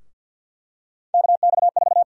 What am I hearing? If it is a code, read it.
Morse code, heard as a keyed tone, about 48 wpm, 700 Hz. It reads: KX4
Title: USA Call Sign Prefixes - Repeated 2x - 50wpm